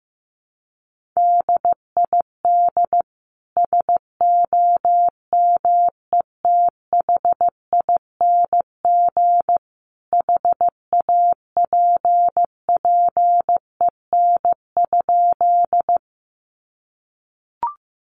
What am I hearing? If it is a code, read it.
Morse code, heard as a keyed tone, about 15 wpm, 700 Hz. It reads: DID SOMETHING HAPPEN?
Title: Sentences from Top 600 Words 15wpm